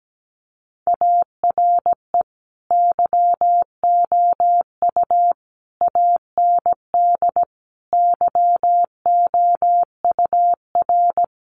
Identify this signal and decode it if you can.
Morse code, heard as a keyed tone, about 17 wpm, 700 Hz. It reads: ARE YOU AND YOUR
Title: Sentences from Top 1400 Words 17wpm